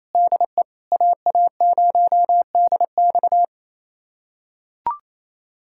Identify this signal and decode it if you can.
Morse code, heard as a keyed tone, about 28 wpm, 700 Hz. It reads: DE AA0DX
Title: USA Callsign DE Callsign 28wpm